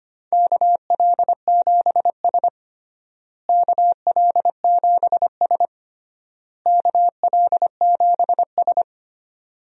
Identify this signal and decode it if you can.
Morse code, heard as a keyed tone, about 25 wpm, 700 Hz. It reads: KL7S KL7S KL7S
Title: DXCC-100 Call Signs - 4 Characters - Repeated 3x - 25wpm